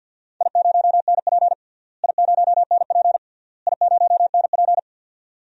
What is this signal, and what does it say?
Morse code, heard as a keyed tone, about 48 wpm, 700 Hz. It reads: I0NP I0NP I0NP
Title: DXCC-100 Call Signs - 4 Characters - Repeated 3x - 50wpm